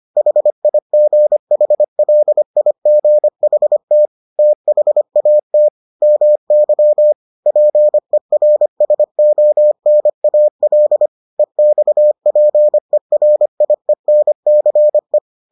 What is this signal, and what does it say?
Morse code, heard as a keyed tone, about 25 wpm, 600 Hz. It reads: HIGHLIGHT THAT MY PERSONAL EXPERIENCE